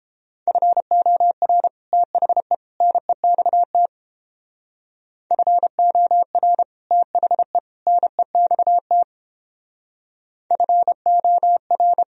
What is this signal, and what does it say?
Morse code, heard as a keyed tone, about 31 wpm, 700 Hz. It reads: FOR THE NEXT FOR THE NEXT FOR